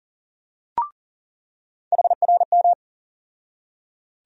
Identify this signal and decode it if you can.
Morse code, heard as a keyed tone, about 40 wpm, 700 Hz. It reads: HRM
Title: USA Call Sign Postfixes - 3 Letters 40wpm